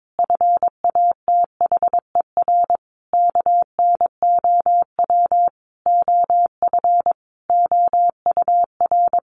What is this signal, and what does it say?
Morse code, heard as a keyed tone, about 22 wpm, 700 Hz. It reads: FATHER KNOW OF OUR
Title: Sentences from Top 1500 Words 22wpm